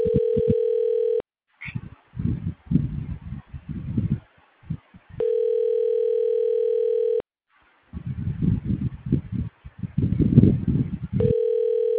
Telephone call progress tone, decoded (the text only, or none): ringback tone